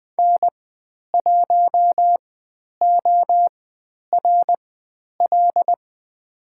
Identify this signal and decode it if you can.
Morse code, heard as a keyed tone, about 20 wpm, spaced as for 10 wpm, 700 Hz. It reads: N1ORL
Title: 095 Lesson 16g - Letter c - Call Signs 20wpm